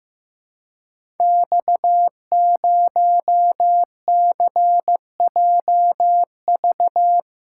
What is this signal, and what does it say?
Morse code, heard as a keyed tone, about 15 wpm, 700 Hz. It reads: X0CJV
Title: Code Groups - 5 Letter:Numbers - 15wpm